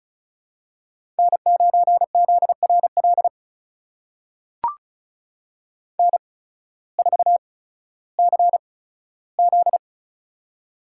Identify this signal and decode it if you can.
Morse code, heard as a keyed tone, about 36 wpm, spaced as for 16 wpm, 700 Hz. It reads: N9ZRL N4CZ
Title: USA Call Signs - 8x Character Spacing - 35wpm